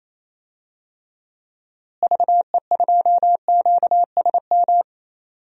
Morse code, heard as a keyed tone, about 27 wpm, 700 Hz. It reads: VE2QSM
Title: DXCC-100 Call Signs - 6 Characters - Repeated 2x - 28wpm